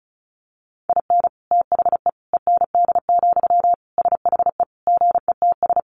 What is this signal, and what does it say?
Morse code, heard as a keyed tone, about 34 wpm, 700 Hz. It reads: IN THE END, SHE GETS
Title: Additional Easy Phrases 35wpm